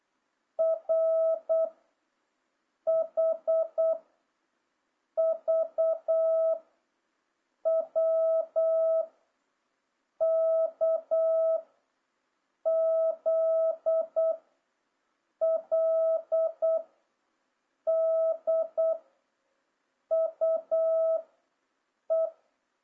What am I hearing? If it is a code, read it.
Morse code, heard as a keyed tone, about 8 wpm, 650 Hz. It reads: R H V W K Z L D U E
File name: jh8cbh / 2026cw